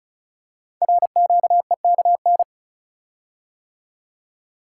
Morse code, heard as a keyed tone, about 36 wpm, 700 Hz. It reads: RQEKN
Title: Code Groups - 5 Letters - 35wpm